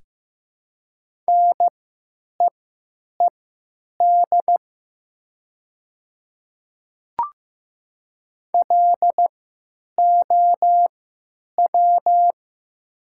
Morse code, heard as a keyed tone, about 15 wpm, spaced as for 9 wpm, 700 Hz. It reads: NEED LOW
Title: 3x Character Spacing - Top 200 Words - 15wpm